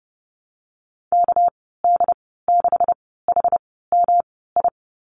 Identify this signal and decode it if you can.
Morse code, heard as a keyed tone, about 30 wpm, spaced as for 17 wpm, 700 Hz. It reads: KD6HMI
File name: Lesson 37g - Letter x - Call Signs 30wpm